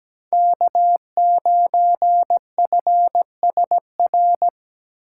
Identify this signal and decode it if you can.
Morse code, heard as a keyed tone, about 20 wpm, 700 Hz. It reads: K9FSR
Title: Speed Racing - USA 1x3 Call Signs 17wpm